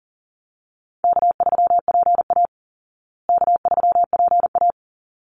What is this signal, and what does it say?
Morse code, heard as a keyed tone, about 40 wpm, 700 Hz. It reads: K3PA K3PA
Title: Easy Call Signs Repeated 2x - 40wpm